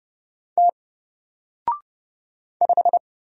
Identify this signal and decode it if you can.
Morse code, heard as a keyed tone, about 30 wpm, 700 Hz. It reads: T 5
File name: Single-Letter-Number-30wpm